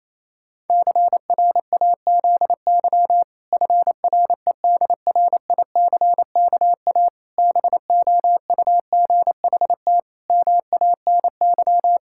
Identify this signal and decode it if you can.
Morse code, heard as a keyed tone, about 28 wpm, 700 Hz. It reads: CRAZY FREDRICKA BOUGHT MANY
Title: Pangram Challenge v2 - 28wpm